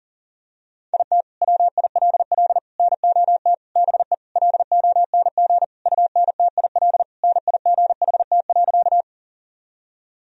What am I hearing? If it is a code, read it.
Morse code, heard as a keyed tone, about 40 wpm, 700 Hz. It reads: IT WILL NOT BE LONG UNTIL NIGHT.
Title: Sentences from Top 1000 Words - Review Part 1 - 40wpm